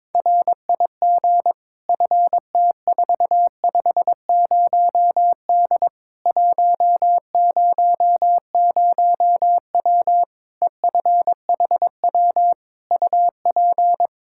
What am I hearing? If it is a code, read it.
Morse code, heard as a keyed tone, about 22 wpm, 700 Hz. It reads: RIG FT450D 100W EFHW UP
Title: K1USN SST QSO Practice 22wpm